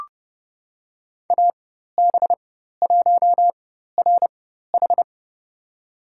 Morse code, heard as a keyed tone, about 30 wpm, spaced as for 14 wpm, 700 Hz. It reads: AB1RH
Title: USA Call Signs - 5 Characters - 4x Character Spacing - 30wpm